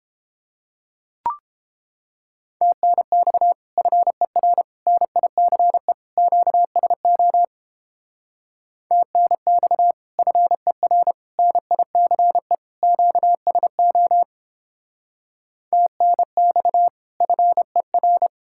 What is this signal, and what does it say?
Morse code, heard as a keyed tone, about 31 wpm, 700 Hz. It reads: TNX FER NICE QSO TNX FER NICE QSO TNX FER